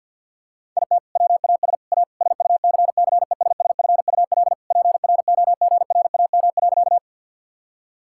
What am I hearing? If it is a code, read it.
Morse code, heard as a keyed tone, about 50 wpm, 700 Hz. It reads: IT WAS A SUCCESSFUL PROGRAM.
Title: Sentences from Top 1000 Words - Review Part 5 - 50wpm